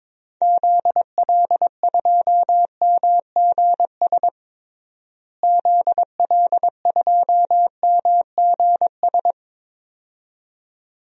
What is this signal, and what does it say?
Morse code, heard as a keyed tone, about 22 wpm, 700 Hz. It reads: ZL2MGS ZL2MGS
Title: DXCC-100 Call Signs - 6 Characters - Repeated 2x - 22wpm